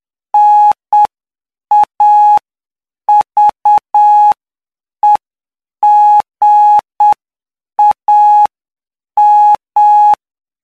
Morse code, heard as a keyed tone, about 8 wpm, spaced as for 7 wpm, 840 Hz. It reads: NAVEGAM
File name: cw / py2lcd / 13-PY2LCD-Li��o 22